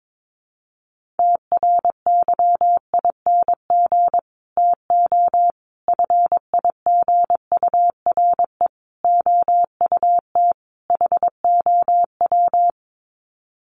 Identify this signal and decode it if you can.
Morse code, heard as a keyed tone, about 22 wpm, 700 Hz. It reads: TRYING TO FIGURE OUT HOW
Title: Sets of 5 Words Encore 22wpm